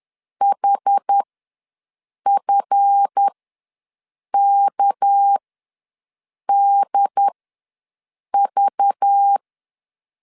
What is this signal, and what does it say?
Morse code, heard as a keyed tone, about 10 wpm, spaced as for 6 wpm, 800 Hz. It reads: HFKDV
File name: LEZIONI / 8 YJ45 / 8 YJ4 ws 10wpm 4020